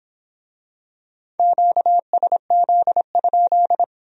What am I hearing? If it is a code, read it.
Morse code, heard as a keyed tone, about 24 wpm, 700 Hz. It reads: QSZ?